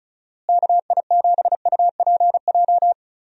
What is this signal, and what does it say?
Morse code, heard as a keyed tone, about 34 wpm, 700 Hz. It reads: KI7UPJ
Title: USA Call Signs - 6 Characters - Repeated 2x - 35wpm